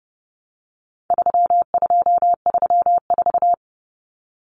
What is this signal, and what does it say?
Morse code, heard as a keyed tone, about 30 wpm, 700 Hz. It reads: 3234
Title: Common Cut Numbers - 4-Digit 30wpm